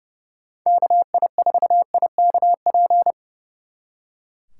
Morse code, heard as a keyed tone, about 36 wpm, 700 Hz. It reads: KI4IKP
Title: Speed Racing - USA 2x3 Call Signs 30wpm